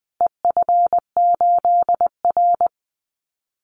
Morse code, heard as a keyed tone, about 20 wpm, 700 Hz. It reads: EF8R
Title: Call Signs Repeated 2x - 20wpm